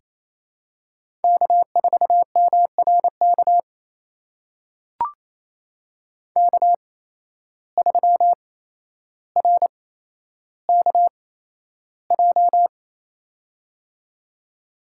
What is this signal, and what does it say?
Morse code, heard as a keyed tone, about 28 wpm, spaced as for 12 wpm, 700 Hz. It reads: K4MRK K3RKJ
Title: USA Call Signs - 8x Character Spacing - 28wpm